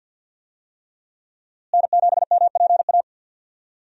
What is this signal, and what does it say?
Morse code, heard as a keyed tone, about 52 wpm, 700 Hz. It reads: N7MPA
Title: USA Call Signs - 5 Characters - Repeated 3x - 50wpm